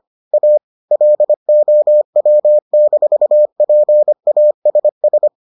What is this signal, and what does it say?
Morse code, heard as a keyed tone, about 25 wpm, 600 Hz. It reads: A LOW-PASS